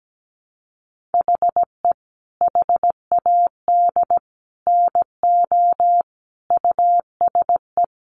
Morse code, heard as a keyed tone, about 17 wpm, 700 Hz. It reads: HE HAD NO USE
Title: Sentences from Top 1200 Words 17wpm